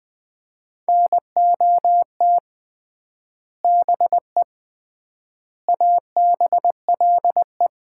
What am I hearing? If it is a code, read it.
Morse code, heard as a keyed tone, about 21 wpm, 700 Hz. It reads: NOT BE ABLE
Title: 3x Word-Spacing - Sets of 4 Words - 20wpm